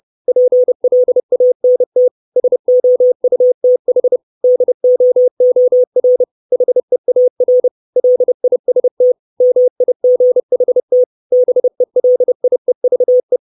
Morse code, heard as a keyed tone, about 30 wpm, 500 Hz. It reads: PLANT SOUTH DOOR HEAR LIST MIGHT BELIEVE
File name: 632 Words 30WPM